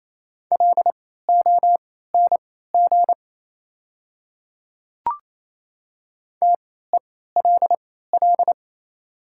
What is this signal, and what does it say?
Morse code, heard as a keyed tone, about 28 wpm, spaced as for 16 wpm, 700 Hz. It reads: LONG TELL
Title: 3x Character Spacing - Top 200 Words - 28wpm